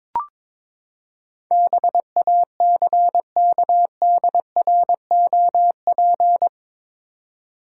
Morse code, heard as a keyed tone, about 22 wpm, 700 Hz. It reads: BACKDROP
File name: practice / 22wpm / Eight Letter Words - 22wpm